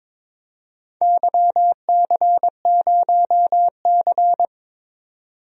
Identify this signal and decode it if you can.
Morse code, heard as a keyed tone, about 23 wpm, 700 Hz. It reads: YC0C
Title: DXCC-100 Call Signs - 4 Characters - 22wpm